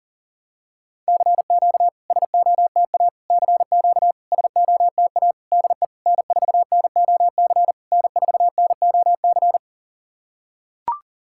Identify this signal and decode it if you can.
Morse code, heard as a keyed tone, about 40 wpm, 700 Hz. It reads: CQ SOTA CQ SOTA DE N4NOC N4NOC
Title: SOTA QSO Practice 40wpm